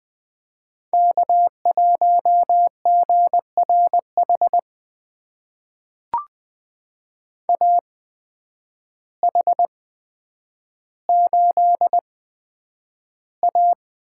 Morse code, heard as a keyed tone, about 20 wpm, spaced as for 9 wpm, 700 Hz. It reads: K1GRH AH8A
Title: USA Call Signs - 8x Character Spacing - 20wpm